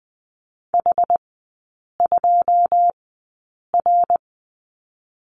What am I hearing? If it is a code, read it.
Morse code, heard as a keyed tone, about 20 wpm, spaced as for 8 wpm, 700 Hz. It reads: H2R
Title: 172 Lesson 28e - Number 9 - Three Characters 20wpm